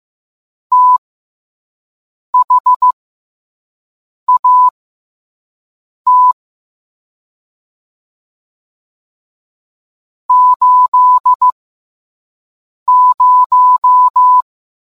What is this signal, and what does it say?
Morse code, heard as a keyed tone, about 15 wpm, spaced as for 5 wpm, 1000 Hz. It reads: THAT 80